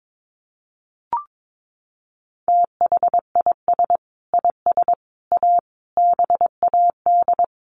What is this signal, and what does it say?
Morse code, heard as a keyed tone, about 22 wpm, 700 Hz. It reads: THIS IS A BAD
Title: Sentences from Top 1400 Words 22wpm